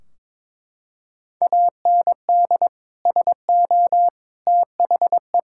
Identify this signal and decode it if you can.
Morse code, heard as a keyed tone, about 22 wpm, 700 Hz. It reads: AND SO THE
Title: Sentences from Top 1600 Words 22wpm